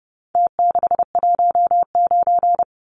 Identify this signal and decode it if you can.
Morse code, heard as a keyed tone, about 30 wpm, 700 Hz. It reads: T619
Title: Numbers - Contest Style - 4-Digit 30wpm